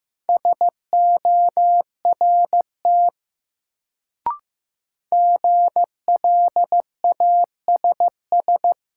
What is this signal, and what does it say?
Morse code, heard as a keyed tone, about 15 wpm, 700 Hz. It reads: SORT GLASS
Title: Sentences from Top 1000 Words - Review Part 3 - New Word Drill - 15wpm